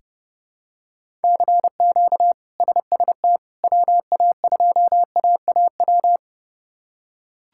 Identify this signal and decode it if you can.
Morse code, heard as a keyed tone, about 30 wpm, 700 Hz. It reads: CQ SST WA2AAW
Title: K1USN SST QSO Practice - 30wpm